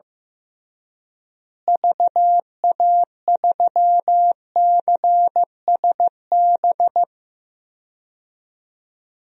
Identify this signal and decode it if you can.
Morse code, heard as a keyed tone, about 15 wpm, 700 Hz. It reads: VA3CSB
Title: DXCC-100 Call Signs - 6 Characters - Repeated 3x - 15wpm